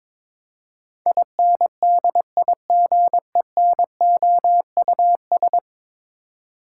Morse code, heard as a keyed tone, about 22 wpm, 700 Hz. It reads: INDIGENOUS
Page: morse-code-ninja-practice-22wpm